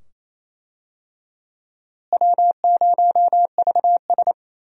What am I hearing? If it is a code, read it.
Morse code, heard as a keyed tone, about 28 wpm, 700 Hz. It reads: W0VS
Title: USA Call Signs - 4 Characters - Repeated 2x - 28wpm